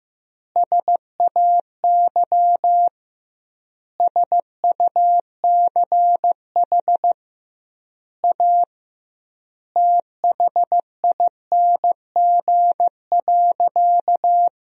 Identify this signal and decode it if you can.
Morse code, heard as a keyed tone, about 15 wpm, 700 Hz. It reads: SAY SUCH A THING.
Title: 2x Word-Spacing - Sentences from Top 200 Words - 15wpm